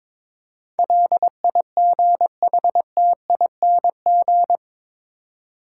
Morse code, heard as a keyed tone, about 22 wpm, 700 Hz. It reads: LIGHTING
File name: practice / 22wpm / Eight Letter Words - 22wpm